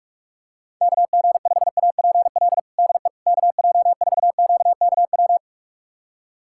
Morse code, heard as a keyed tone, about 46 wpm, 700 Hz. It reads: KG5APL DE KJ4QKW
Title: USA Callsign DE Callsign 45wpm